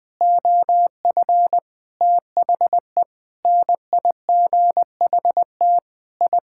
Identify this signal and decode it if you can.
Morse code, heard as a keyed tone, about 20 wpm, 700 Hz. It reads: OF THE NIGHT I
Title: Sentences with 5-Word Phrases - Spoken Beginning - 20wpm